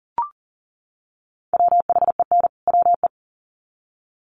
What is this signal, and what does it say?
Morse code, heard as a keyed tone, about 40 wpm, 700 Hz. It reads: WHEN WE
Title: Sets of 2 Words 40wpm